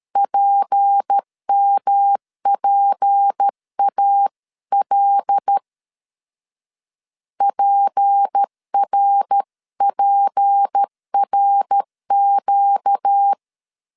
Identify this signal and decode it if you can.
Morse code, heard as a keyed tone, about 13 wpm, 800 Hz. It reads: PMPAL PRPRQ
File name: LEZIONI / 6 PKX0 / 6 P 12wpm 2033